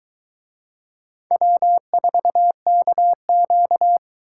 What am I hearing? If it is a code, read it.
Morse code, heard as a keyed tone, about 22 wpm, 700 Hz. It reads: W4KQ